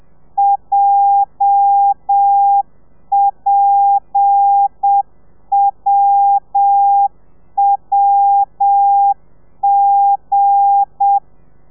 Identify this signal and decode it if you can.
Morse code, heard as a keyed tone, about 7 wpm, 800 Hz. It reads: JPWWG
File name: telegrafia / i2tfj / Lez8 35